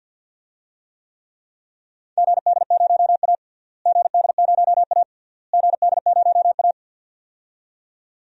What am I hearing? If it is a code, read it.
Morse code, heard as a keyed tone, about 48 wpm, 700 Hz. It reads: GD0A GD0A GD0A
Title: DXCC-100 Call Signs - 4 Characters - Repeated 3x - 50wpm